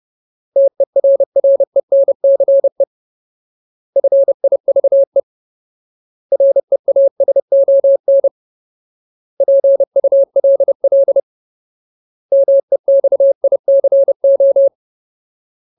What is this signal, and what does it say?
Morse code, heard as a keyed tone, about 30 wpm, 550 Hz. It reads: TERRENCE FIVE REASON PULL MEXICO
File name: Ultimate Word List / 30WPM Ultimate